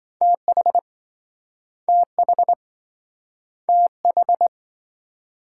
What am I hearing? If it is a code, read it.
Morse code, heard as a keyed tone, about 27 wpm, 700 Hz. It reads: TH TH TH